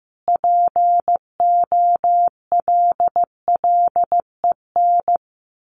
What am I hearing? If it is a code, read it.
Morse code, heard as a keyed tone, about 15 wpm, 700 Hz. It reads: POLLEN
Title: A Sound Of Thunder 15wpm